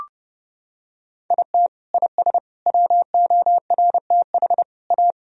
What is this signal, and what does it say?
Morse code, heard as a keyed tone, about 30 wpm, 700 Hz. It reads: IT IS WORTH A
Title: Sentences from Top 1000 Words - Review Part 3 - 30wpm